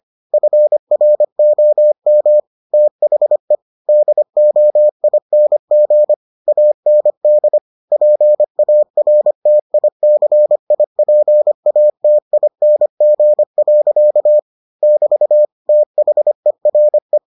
Morse code, heard as a keyed tone, about 25 wpm, 600 Hz. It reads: FROM THE DOING AND PARTICIPATING. <BT> THERE